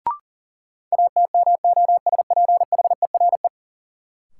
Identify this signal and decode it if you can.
Morse code, heard as a keyed tone, about 40 wpm, 700 Hz. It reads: ATMOSPHERE
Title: Sentences from Top 1600 Words - New Word Drill Alt - 40wpm